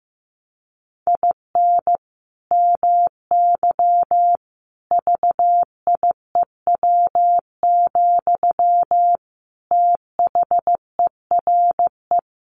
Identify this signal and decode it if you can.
Morse code, heard as a keyed tone, about 15 wpm, 700 Hz. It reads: IN MY VIEW, THERE
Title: Sentences from Top 1500 Words 15wpm